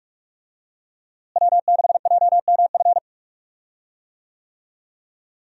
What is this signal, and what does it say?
Morse code, heard as a keyed tone, about 46 wpm, 700 Hz. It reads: W6JMF
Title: USA Call Signs - 5 Characters - Repeated 3x - 45wpm